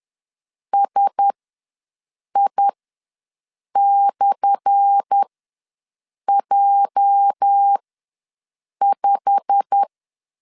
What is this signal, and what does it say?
Morse code, heard as a keyed tone, about 10 wpm, spaced as for 6 wpm, 800 Hz. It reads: SI/J5
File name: LEZIONI / 10 SIGNS / 10A SLASH ws 10wpm 4121